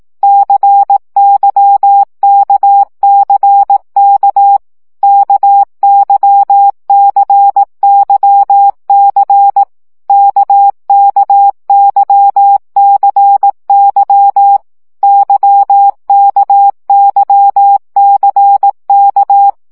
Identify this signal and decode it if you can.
Morse code, heard as a keyed tone, about 18 wpm, 800 Hz. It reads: CYKCK KYCYC KKYCY YKYCK
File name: CW gyakorlas k c y 15 baud